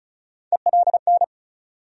Morse code, heard as a keyed tone, about 36 wpm, 700 Hz. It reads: ELN
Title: USA Call Sign Postfixes - 3 Letters 35wpm